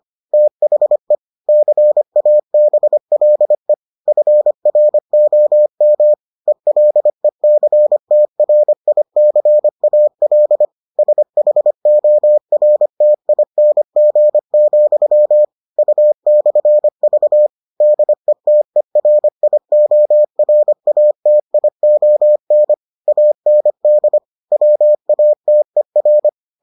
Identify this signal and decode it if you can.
Morse code, heard as a keyed tone, about 25 wpm, 600 Hz. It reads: THE CABLE FROM ELECTRICAL SHORTING, U/V DETERIORATION AND WATER